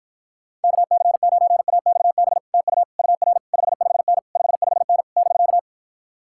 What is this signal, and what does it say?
Morse code, heard as a keyed tone, about 56 wpm, 700 Hz. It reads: KC9AXB TU UR 55N 55N <BK>